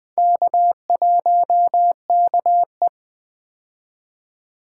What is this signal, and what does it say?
Morse code, heard as a keyed tone, about 20 wpm, 700 Hz. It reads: K1KE
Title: Four Letter-Number Combinations - 20wpm